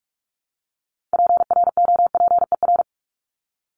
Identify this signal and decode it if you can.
Morse code, heard as a keyed tone, about 44 wpm, 700 Hz. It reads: PROPER
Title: Top 751-1000 Words 45wpm